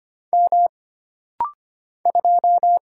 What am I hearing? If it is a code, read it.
Morse code, heard as a keyed tone, about 25 wpm, 700 Hz. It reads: M 2
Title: Single Letter-Number Alt - Rapid-Fire 25wpm